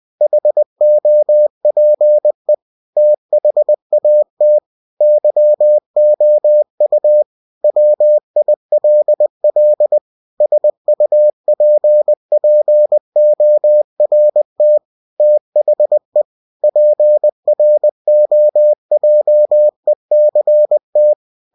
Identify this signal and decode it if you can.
Morse code, heard as a keyed tone, about 20 wpm, 600 Hz. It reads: HOPE THAT YOU WILL SUPPORT THE PROJECT
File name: cw-books / winnie-the-pooh / winnie-20wpm-page-0114